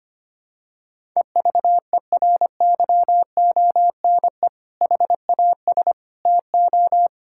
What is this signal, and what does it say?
Morse code, heard as a keyed tone, about 25 wpm, 700 Hz. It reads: EVERYONE HAS TO
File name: practice / 25wpm / Sentences with 2-Word Phrases - 25wpm